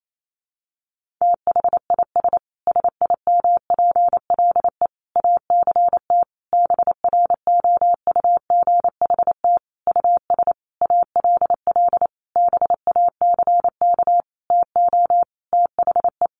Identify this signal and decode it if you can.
Morse code, heard as a keyed tone, about 28 wpm, 700 Hz. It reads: THIS SIMPLE ACT BROUGHT US ALL BACK TO THE
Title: Sentences from Top 1500 Words 28wpm